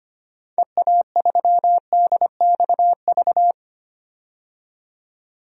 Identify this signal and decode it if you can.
Morse code, heard as a keyed tone, about 25 wpm, 700 Hz. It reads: EA3DXV
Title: DXCC-100 Call Signs - 6 Characters - 25wpm